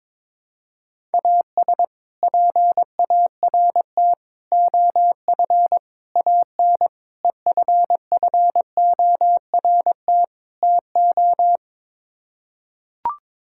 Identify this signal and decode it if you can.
Morse code, heard as a keyed tone, about 22 wpm, 700 Hz. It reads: AS PART OF AN EFFORT TO
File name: practice / 22wpm / Sets of 6 Words - 22wpm